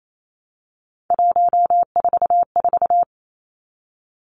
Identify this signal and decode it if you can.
Morse code, heard as a keyed tone, about 28 wpm, 700 Hz. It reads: 144
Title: Three Challenging Letter-Number Combinations 28wpm